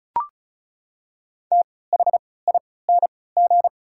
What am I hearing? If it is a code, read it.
Morse code, heard as a keyed tone, about 36 wpm, spaced as for 20 wpm, 700 Hz. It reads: THING
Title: 3x Character Spacing - Top 100 Words - 35wpm